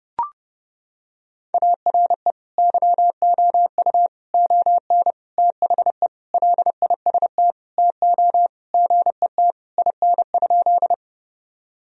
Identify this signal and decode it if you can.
Morse code, heard as a keyed tone, about 30 wpm, 700 Hz. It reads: ARE YOU ON THE LIST TO GET IN?
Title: Sentences from Top 1000 Words - Review Part 2 - 30wpm